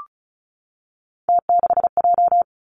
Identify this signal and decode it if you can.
Morse code, heard as a keyed tone, about 36 wpm, 700 Hz. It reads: T6J
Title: Three Challenging Letter-Number Combinations 35wpm